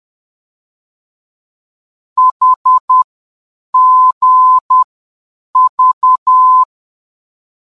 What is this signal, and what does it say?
Morse code, heard as a keyed tone, about 10 wpm, 1020 Hz. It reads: H G V